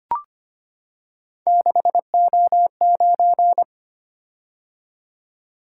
Morse code, heard as a keyed tone, about 25 wpm, 700 Hz. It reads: 6O9
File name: Three Letter-Number Combinations - 25wpm